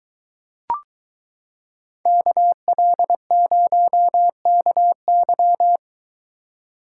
Morse code, heard as a keyed tone, about 23 wpm, 700 Hz. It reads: KL0KY